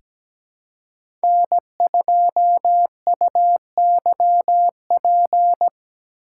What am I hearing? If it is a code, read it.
Morse code, heard as a keyed tone, about 17 wpm, 700 Hz. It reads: N2UYP
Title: USA Call Signs - 3x Character Spacing - 17wpm